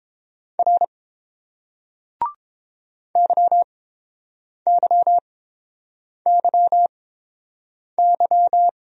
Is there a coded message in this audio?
Yes: Morse code from 0.6 to 8.7 s, heard as a keyed tone, about 31 wpm, spaced as for 7 wpm, 700 Hz. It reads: R YYYY